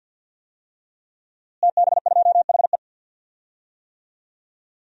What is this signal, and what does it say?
Morse code, heard as a keyed tone, about 47 wpm, 700 Hz. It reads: TB2HE
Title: DXCC-100 Call Signs - 5 Characters - Repeated 3x - 50wpm